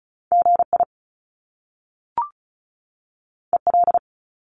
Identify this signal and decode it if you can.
Morse code, heard as a keyed tone, about 36 wpm, 700 Hz. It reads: GI EL
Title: Top Two Letter Combinations in Words 35wpm